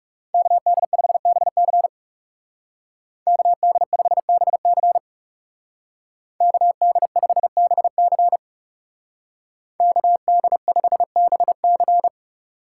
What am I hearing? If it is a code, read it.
Morse code, heard as a keyed tone, about 34 wpm, 700 Hz. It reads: KD5BC KD5BC KD5BC KD5BC
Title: Speed Racing - USA 2x2 Call Signs 30wpm